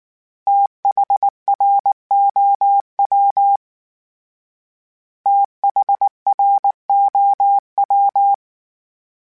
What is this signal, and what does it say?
Morse code, heard as a keyed tone, about 19 wpm, 800 Hz. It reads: THROW THROW
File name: THs / TH / TH Prefixes 3 @18